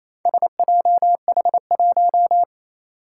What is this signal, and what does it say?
Morse code, heard as a keyed tone, about 29 wpm, 700 Hz. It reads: SJH1
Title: Four Challenging Letter-Number Combinations 28wpm